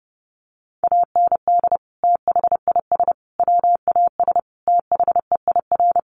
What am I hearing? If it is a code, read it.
Morse code, heard as a keyed tone, about 30 wpm, 700 Hz. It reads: AND THIS WAS THEIR
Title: Sentences from Top 1100 Words 30wpm